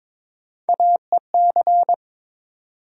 Morse code, heard as a keyed tone, about 22 wpm, 700 Hz. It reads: AEC
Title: USA Call Sign Postfixes - 3 Letters 22wpm